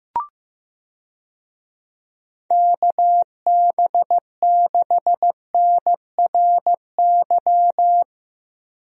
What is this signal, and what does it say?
Morse code, heard as a keyed tone, about 15 wpm, 700 Hz. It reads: KB6NRY
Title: USA Call Signs - 6 Characters - Repeated 3x - 15wpm